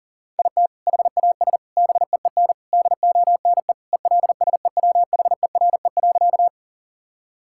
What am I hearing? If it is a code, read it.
Morse code, heard as a keyed tone, about 40 wpm, 700 Hz. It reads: IT HAS BEEN DONE ELSEWHERE.
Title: Sentences from Top 1600 Words 40wpm